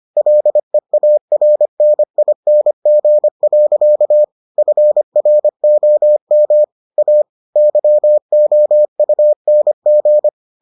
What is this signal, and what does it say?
Morse code, heard as a keyed tone, about 25 wpm, 600 Hz. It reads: LEARNING. FROM A YOUNG